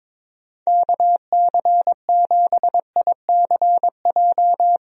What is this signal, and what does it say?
Morse code, heard as a keyed tone, about 22 wpm, 700 Hz. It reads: KC7ICJ
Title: USA Call Signs - 6 Characters - Repeated 3x - 22wpm